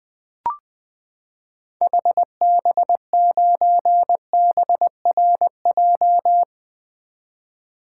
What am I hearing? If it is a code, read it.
Morse code, heard as a keyed tone, about 20 wpm, 700 Hz. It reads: HB9BRJ
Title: 2023 SOTA Activator Call Signs - 20wpm